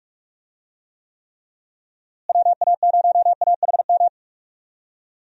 Morse code, heard as a keyed tone, about 46 wpm, 700 Hz. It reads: WA0AHM
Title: USA Call Signs - 6 Characters - Repeated 3x - 45wpm